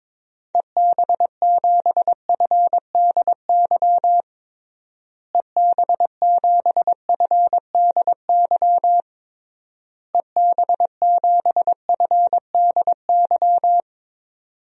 Morse code, heard as a keyed tone, about 22 wpm, 700 Hz. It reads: EB7FDY EB7FDY EB7FDY
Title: DXCC-100 Call Signs - 6 Characters - Repeated 3x - 22wpm